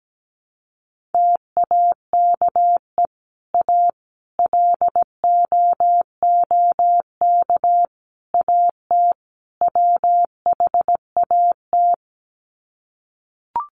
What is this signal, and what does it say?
Morse code, heard as a keyed tone, about 17 wpm, 700 Hz. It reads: TAKE A LOOK AT WHAT
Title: Sets of 5 Words Encore2 17wpm